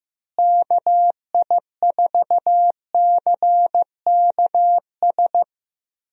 Morse code, heard as a keyed tone, about 15 wpm, 700 Hz. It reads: KI4CKS
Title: USA Call Signs - 6 Characters - Repeated 3x - 15wpm